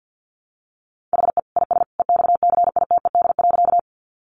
Morse code, heard as a keyed tone, about 50 wpm, 700 Hz. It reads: HE IS EXCITED.
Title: Additional Easy Phrases 50wpm